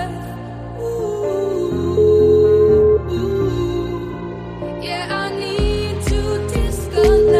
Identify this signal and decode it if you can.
Ringback tone (Europe, 425 Hz).